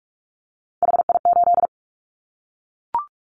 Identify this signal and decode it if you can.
Morse code, heard as a keyed tone, about 44 wpm, 700 Hz. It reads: HI8
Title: Callsign Prefixes for Top 100 DXCC - 45wpm